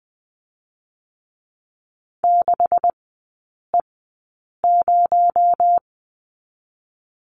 Morse code, heard as a keyed tone, about 20 wpm, spaced as for 8 wpm, 700 Hz. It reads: 6E0
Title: 218 Lesson 35e - Number 0 - Three Characters 20wpm